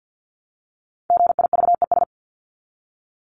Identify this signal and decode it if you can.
Morse code, heard as a keyed tone, about 48 wpm, 700 Hz. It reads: GIVES
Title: ES:S Suffix - Top 100 Words with Base Words 50wpm